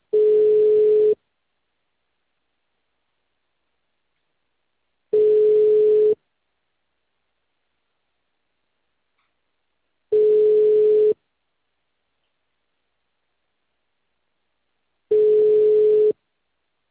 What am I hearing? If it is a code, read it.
Ringback tone (Europe, 425 Hz).